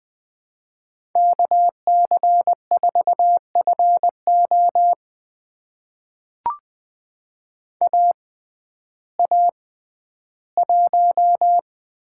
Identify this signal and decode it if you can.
Morse code, heard as a keyed tone, about 21 wpm, spaced as for 12 wpm, 700 Hz. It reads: KC4FO AA1
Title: USA Call Signs - 6x Character Spacing - 20wpm